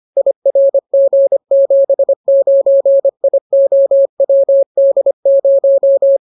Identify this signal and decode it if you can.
Morse code, heard as a keyed tone, about 25 wpm, 550 Hz. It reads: IRG79IOWD0